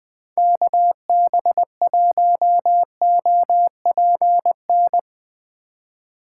Morse code, heard as a keyed tone, about 20 wpm, 700 Hz. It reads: KB1OPN